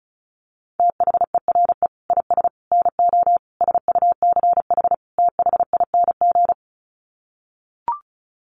Morse code, heard as a keyed tone, about 34 wpm, 700 Hz. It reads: THERE IS NO SUCH THING
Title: Sets of 5 Words Encore 35wpm